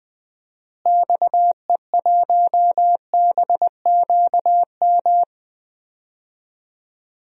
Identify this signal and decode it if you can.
Morse code, heard as a keyed tone, about 21 wpm, 700 Hz. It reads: XE1BQM
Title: DXCC-100 Call Signs - 6 Characters - Repeated 2x - 20wpm